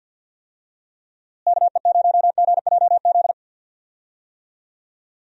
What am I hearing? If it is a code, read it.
Morse code, heard as a keyed tone, about 52 wpm, 700 Hz. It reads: KE0GJZ
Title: USA Call Signs - 6 Characters - Repeated 2x - 50wpm